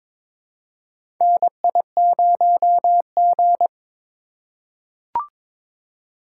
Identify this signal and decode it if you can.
Morse code, heard as a keyed tone, about 22 wpm, 700 Hz. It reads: NI0G
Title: USA Call Signs - 3x Character Spacing - 22wpm